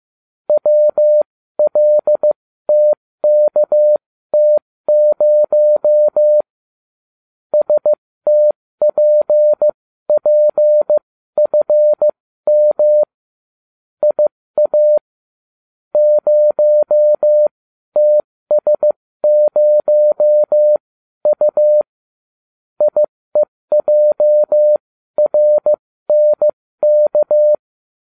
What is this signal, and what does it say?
Morse code, heard as a keyed tone, about 15 wpm, spaced as for 13 wpm, 600 Hz. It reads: WLTKT0 STPPFM IA 0TS0U IEJRNK